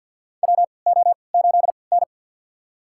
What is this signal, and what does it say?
Morse code, heard as a keyed tone, about 48 wpm, spaced as for 28 wpm, 700 Hz. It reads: WO8N
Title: USA Call Signs - 4 Characters - 3x Character Spacing - 50wpm